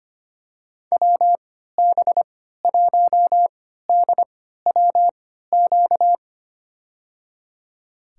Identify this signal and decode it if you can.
Morse code, heard as a keyed tone, about 25 wpm, spaced as for 14 wpm, 700 Hz. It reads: WB1DWQ
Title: USA Call Signs - 6 Characters - 3x Character Spacing - 25wpm